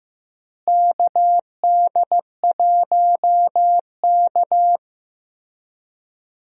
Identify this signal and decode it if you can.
Morse code, heard as a keyed tone, about 15 wpm, 700 Hz. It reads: KD1K
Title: USA Call Signs - 4 Characters - Repeated 3x - 15wpm